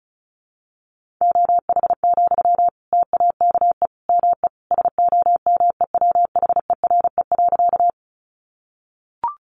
Morse code, heard as a keyed tone, about 34 wpm, 700 Hz. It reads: OH, TAKE ME SOMEWHERE.
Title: Sentences from Top 1400 Words 35wpm